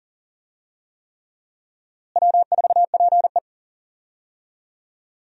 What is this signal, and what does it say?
Morse code, heard as a keyed tone, about 40 wpm, 700 Hz. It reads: W4PE
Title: USA Call Signs - 4 Characters - Repeated 3x - 40wpm